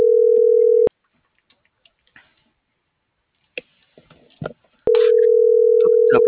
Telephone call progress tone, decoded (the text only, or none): ringback tone